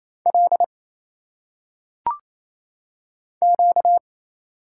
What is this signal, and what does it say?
Morse code, heard as a keyed tone, about 27 wpm, 700 Hz. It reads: L Q
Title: Single Letters - Phonetic Pronunciation 28wpm